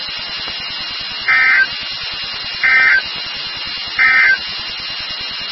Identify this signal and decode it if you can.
Emergency Alert System (SAME) tones, heard as harsh buzzing bursts. It is heard from 1.2 to 4.4 s.
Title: NOAA Weather Radio WXK36